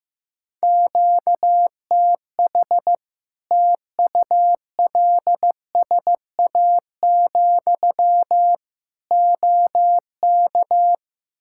Morse code, heard as a keyed tone, about 15 wpm, 700 Hz. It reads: QTH TULSA, OK
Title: Practice QSO - Exchange 1 - Beginning - 15wpm